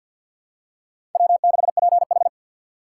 Morse code, heard as a keyed tone, about 52 wpm, 700 Hz. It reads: W6PH
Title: USA Call Signs - 2x Character Spacing - 50wpm